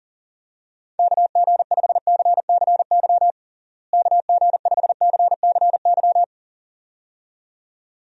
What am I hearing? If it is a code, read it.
Morse code, heard as a keyed tone, about 37 wpm, 700 Hz. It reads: KG5CCY KG5CCY
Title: USA Call Signs - 6 Characters - Repeated 2x - 40wpm